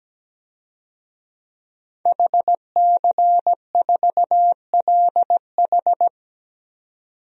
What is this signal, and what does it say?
Morse code, heard as a keyed tone, about 17 wpm, 700 Hz. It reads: HC4LH
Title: DXCC-100 Call Signs - 5 Characters - 17wpm